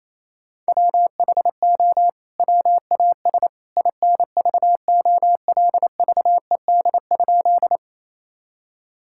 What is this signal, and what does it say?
Morse code, heard as a keyed tone, about 28 wpm, 700 Hz. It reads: WHO WAS INVOLVED?
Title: Sentences from Top 1000 Words - Review Part 5 - 28wpm